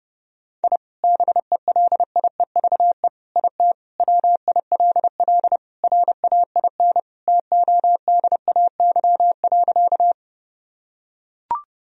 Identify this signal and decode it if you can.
Morse code, heard as a keyed tone, about 32 wpm, 700 Hz. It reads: I BELIEVE IT WILL RAIN TODAY.
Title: Sentences from Top 1000 Words - Review Part 1 - 30wpm